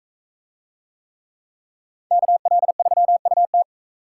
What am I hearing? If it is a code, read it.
Morse code, heard as a keyed tone, about 40 wpm, 700 Hz. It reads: KL3UT